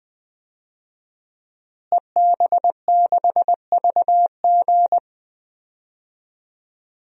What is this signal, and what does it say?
Morse code, heard as a keyed tone, about 20 wpm, 700 Hz. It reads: EB6VG
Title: DXCC-100 Call Signs - 5 Characters - Repeated 2x - 20wpm